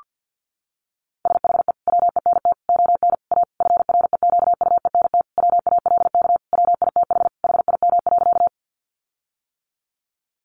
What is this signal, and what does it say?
Morse code, heard as a keyed tone, about 50 wpm, 700 Hz. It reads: SHE WENT ON A FREQUENT WALK WITH HIM.
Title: Sentences from Top 1400 Words 50wpm